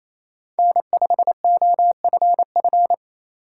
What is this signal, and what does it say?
Morse code, heard as a keyed tone, about 28 wpm, 700 Hz. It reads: N5OFF
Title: USA Call Signs - 5 Characters - 2x Character Spacing - 28wpm